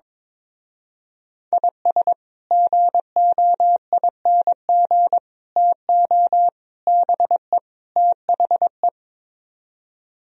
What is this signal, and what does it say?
Morse code, heard as a keyed tone, about 22 wpm, 700 Hz. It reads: IS GOING TO BE THE
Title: Speed-Racing - Sets of 5 Words 22wpm